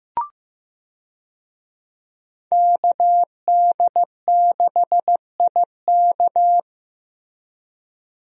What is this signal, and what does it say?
Morse code, heard as a keyed tone, about 15 wpm, 700 Hz. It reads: KD6IK
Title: USA Call Signs - 5 Characters - Repeated 2x - 15wpm